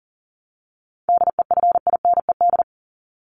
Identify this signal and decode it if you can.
Morse code, heard as a keyed tone, about 40 wpm, 700 Hz. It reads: DEFINED
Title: ED Suffix - Top 100 Words with Base Words 40wpm